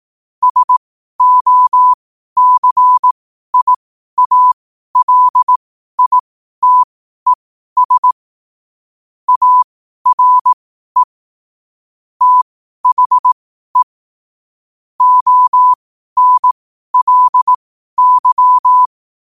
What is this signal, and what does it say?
Morse code, heard as a keyed tone, about 18 wpm, spaced as for 12 wpm, 1000 Hz. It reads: SOCIALITES ARE THE ONLY